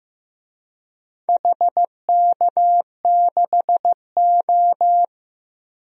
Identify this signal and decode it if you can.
Morse code, heard as a keyed tone, about 15 wpm, 700 Hz. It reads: HK6O
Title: DXCC-100 Call Signs - 4 Characters - Repeated 3x - 15wpm